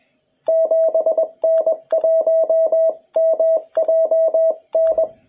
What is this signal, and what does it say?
Morse code, heard as a keyed tone, about 21 wpm, 650 Hz. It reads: 7N1MJN
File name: jh8cbh / 2021cw